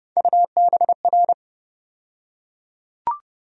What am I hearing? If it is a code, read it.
Morse code, heard as a keyed tone, about 31 wpm, 700 Hz. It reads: UBR
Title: USA Call Sign Postfixes - 3 Letters 30wpm